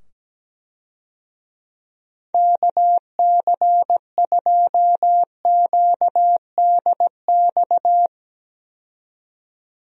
Morse code, heard as a keyed tone, about 17 wpm, 700 Hz. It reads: KC2QDX
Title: USA Call Signs - 6 Characters - 17wpm